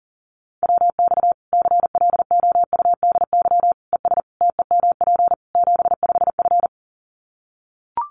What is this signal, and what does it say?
Morse code, heard as a keyed tone, about 40 wpm, 700 Hz. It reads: WX CLOUDY ES TEMP 75F
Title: Common QSO Phrases 40wpm